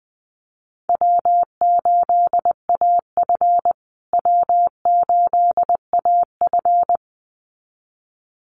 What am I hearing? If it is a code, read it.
Morse code, heard as a keyed tone, about 20 wpm, 700 Hz. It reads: W8AF W8AF
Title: USA Call Signs Repeated 2x 20wpm